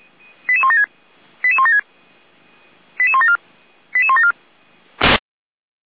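Five-tone selective call (ZVEI), heard as quick runs of five tones, tones of about 70 ms. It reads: 88176 88176 88174 88174
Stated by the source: ZVEI five-tone selective call 88191, received by SDR.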